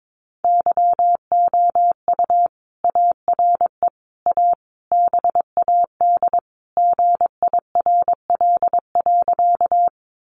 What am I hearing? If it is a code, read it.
Morse code, heard as a keyed tone, about 22 wpm, 700 Hz. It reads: YOU ARE A BAD GIRL.
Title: Sentences from Top 700 Words 22wpm